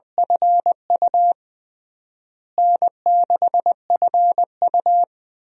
Morse code, heard as a keyed tone, about 21 wpm, 700 Hz. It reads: FU N6FU
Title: USA Call Signs - 4 Characters - Repeated 3x - 20wpm